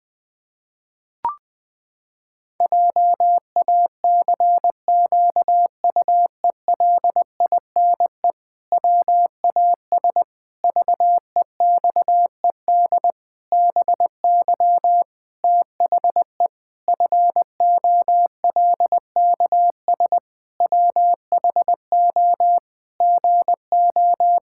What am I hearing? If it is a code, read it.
Morse code, heard as a keyed tone, about 20 wpm, 700 Hz. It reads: JACQUELINE WAS VEXED BY THE FOLKS WHO GO
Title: Pangram Challenge v2 - 20wpm